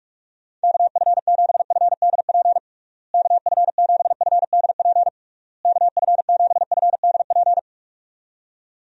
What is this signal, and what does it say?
Morse code, heard as a keyed tone, about 41 wpm, 700 Hz. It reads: KF7FDP KF7FDP KF7FDP
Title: USA Call Signs - 6 Characters - Repeated 3x - 45wpm